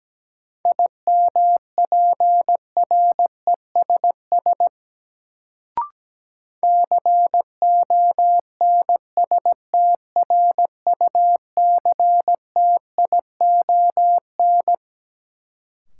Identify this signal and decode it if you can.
Morse code, heard as a keyed tone, about 17 wpm, 700 Hz. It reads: IMPRESS CONSTRUCTION
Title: Sentences from Top 1600 Words - New Word Drill Alt - 17wpm